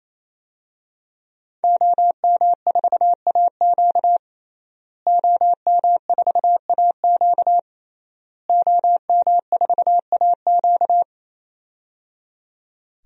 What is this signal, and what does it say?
Morse code, heard as a keyed tone, about 28 wpm, 700 Hz. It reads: OM4AQ OM4AQ OM4AQ
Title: DXCC-100 Call Signs - 5 Characters - Repeated 3x - 28wpm